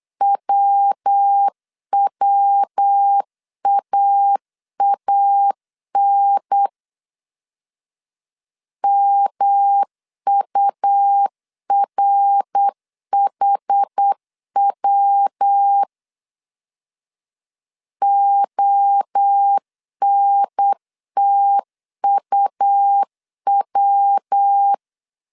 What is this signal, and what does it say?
Morse code, heard as a keyed tone, about 8 wpm, 800 Hz. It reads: WWAAN MURHW ONTUW